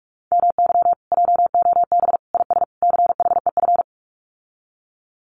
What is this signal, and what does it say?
Morse code, heard as a keyed tone, about 44 wpm, 700 Hz. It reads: MY JOB IS CHEF
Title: Common QSO Phrases Encore 45wpm